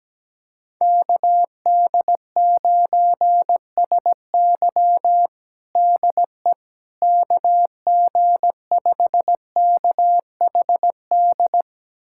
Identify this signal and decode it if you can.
Morse code, heard as a keyed tone, about 17 wpm, 700 Hz. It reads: KD9SY DE KG5KHD
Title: USA Callsign DE Callsign 17wpm